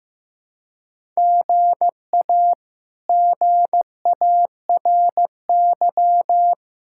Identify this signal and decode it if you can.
Morse code, heard as a keyed tone, about 15 wpm, 700 Hz. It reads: GA GARY
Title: K1USN SST QSO Practice - 15wpm